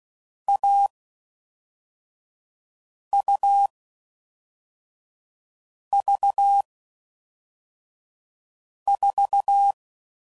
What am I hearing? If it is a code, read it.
Morse code, heard as a keyed tone, about 16 wpm, 800 Hz. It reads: A U V 4
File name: Mixed / Short Runs / AUV AUV4